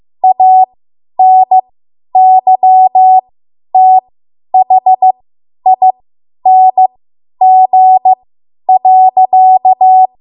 Morse code, heard as a keyed tone, about 15 wpm, spaced as for 10 wpm, 750 Hz. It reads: ANYTHING.
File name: Archive / 10 WPM / 10WPM